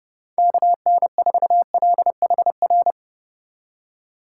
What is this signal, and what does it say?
Morse code, heard as a keyed tone, about 30 wpm, 700 Hz. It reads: KN4LHR